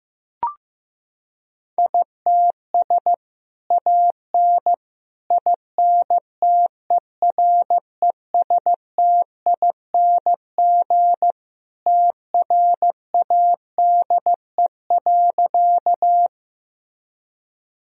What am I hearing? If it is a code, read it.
Morse code, heard as a keyed tone, about 15 wpm, 700 Hz. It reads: ITS AN INTERESTING TRADE.
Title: Sentences from Top 1000 Words - Review Part 4 - 15wpm